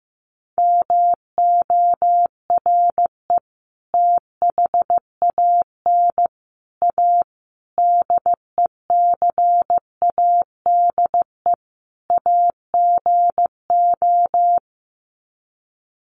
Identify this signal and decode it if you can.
Morse code, heard as a keyed tone, about 15 wpm, 700 Hz. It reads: MORE THAN A DECADE AGO
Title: Sets of 5 Words Encore2 15wpm